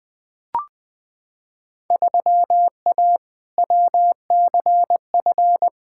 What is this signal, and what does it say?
Morse code, heard as a keyed tone, about 20 wpm, 700 Hz. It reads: 3A WCF
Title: Field Day Exchanges Encore - 20wpm